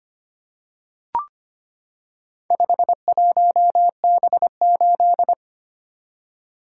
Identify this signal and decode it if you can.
Morse code, heard as a keyed tone, about 25 wpm, 700 Hz. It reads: 51B8
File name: practice / 25wpm / Four Letter-Number Combinations - 25wpm